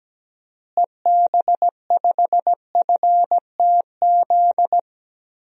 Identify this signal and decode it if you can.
Morse code, heard as a keyed tone, about 17 wpm, 700 Hz. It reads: EB5FTZ
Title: DXCC-100 Call Signs - 6 Characters - Repeated 2x - 17wpm